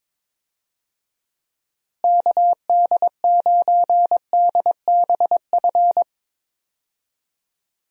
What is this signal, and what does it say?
Morse code, heard as a keyed tone, about 22 wpm, 700 Hz. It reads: KD9DBF
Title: USA Call Signs - 6 Characters - Repeated 2x - 22wpm